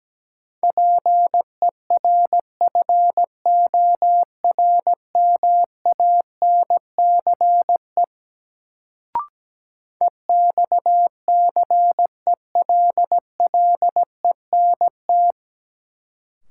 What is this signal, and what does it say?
Morse code, heard as a keyed tone, about 17 wpm, 700 Hz. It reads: PERFORMANCE EXCELLENT
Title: Sentences from Top 1000 Words - New Word Drill - 17wpm